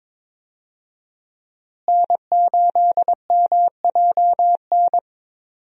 Morse code, heard as a keyed tone, about 22 wpm, 700 Hz. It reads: N8MJN
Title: USA Call Signs - 5 Characters - Repeated 3x - 22wpm